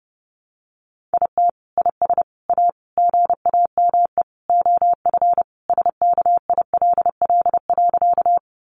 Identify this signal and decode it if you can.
Morse code, heard as a keyed tone, about 30 wpm, 700 Hz. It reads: IT IS A GAME OF SKILL.
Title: Sentences from Top 1200 Words 30wpm